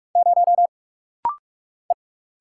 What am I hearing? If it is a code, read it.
Morse code, heard as a keyed tone, about 46 wpm, 700 Hz. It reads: 0 E
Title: Single Letter-Number - Rapid-Fire 45wpm